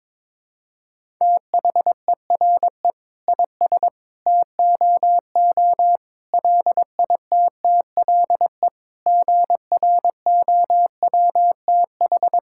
Morse code, heard as a keyed tone, about 22 wpm, 700 Hz. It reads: THERE IS TOO LITTLE GROWTH
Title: Sentences from Top 1000 Words - Review Part 5 - 22wpm